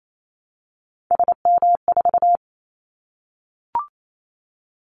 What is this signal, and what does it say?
Morse code, heard as a keyed tone, about 28 wpm, 700 Hz. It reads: SM4
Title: Callsign Prefixes for Top 25 DXCC - Repeated 3x - 28wpm